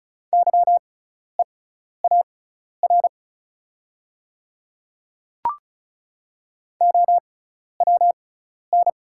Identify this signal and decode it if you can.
Morse code, heard as a keyed tone, about 36 wpm, spaced as for 12 wpm, 690 Hz. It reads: YEAR OWN
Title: 6x Character Spacing - Top 300 Words - 35wpm